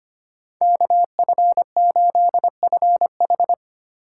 Morse code, heard as a keyed tone, about 31 wpm, 700 Hz. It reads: KF8FH